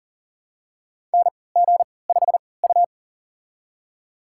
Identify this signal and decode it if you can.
Morse code, heard as a keyed tone, about 37 wpm, spaced as for 22 wpm, 700 Hz. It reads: NG5U
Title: USA Call Signs - 4 Characters - 3x Character Spacing - 40wpm